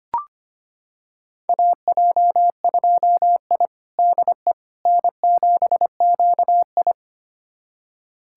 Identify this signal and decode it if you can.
Morse code, heard as a keyed tone, about 25 wpm, 700 Hz. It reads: AJ2I DE N7QI
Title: USA Callsign DE Callsign - 4 Characters - 25wpm